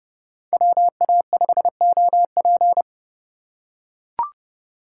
Morse code, heard as a keyed tone, about 26 wpm, 700 Hz. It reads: WA5OP